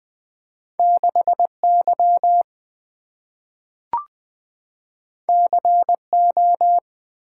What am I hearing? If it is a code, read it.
Morse code, heard as a keyed tone, about 20 wpm, 700 Hz. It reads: 6 Y C O
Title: Two Letter-Number Combinations - 20wpm